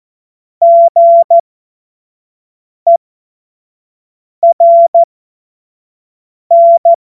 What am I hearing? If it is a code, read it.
Morse code, heard as a keyed tone, about 14 wpm, spaced as for 5 wpm, 680 Hz. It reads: GERN
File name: Morsekurs / Witze / Witz 20. Dezember 2022 25 F70